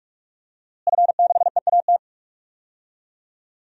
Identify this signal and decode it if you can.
Morse code, heard as a keyed tone, about 46 wpm, 700 Hz. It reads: F6EAT
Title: DXCC-100 Call Signs - 5 Characters - Repeated 3x - 45wpm